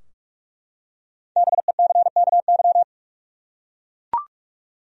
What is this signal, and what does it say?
Morse code, heard as a keyed tone, about 43 wpm, 700 Hz. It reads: BECKY
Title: USA Common Names - 45wpm